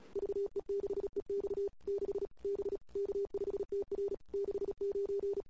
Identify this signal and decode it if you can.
Morse code, heard as a keyed tone, about 35 wpm, 410 Hz. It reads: VE6EX 6 B K5TR 69
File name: CW / 2008-ss-cw-k5tr-op-n5rz -21:00